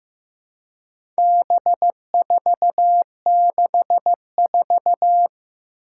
Morse code, heard as a keyed tone, about 15 wpm, 700 Hz. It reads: B464
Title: Four Challenging Letter-Number Combinations 15wpm